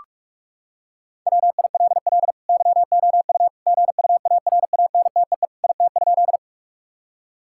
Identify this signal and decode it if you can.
Morse code, heard as a keyed tone, about 41 wpm, 700 Hz. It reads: WILL YOU GUARANTEE IT?
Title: Sentences from Top 1600 Words 45wpm